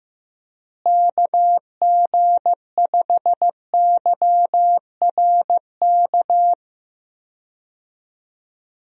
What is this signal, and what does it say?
Morse code, heard as a keyed tone, about 15 wpm, 700 Hz. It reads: KG5YRK
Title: USA Call Signs - 6 Characters - Repeated 3x - 15wpm